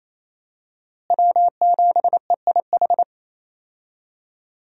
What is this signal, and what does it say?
Morse code, heard as a keyed tone, about 28 wpm, 700 Hz. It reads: W7EIH
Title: USA Call Signs - 5 Characters - Repeated 2x - 28wpm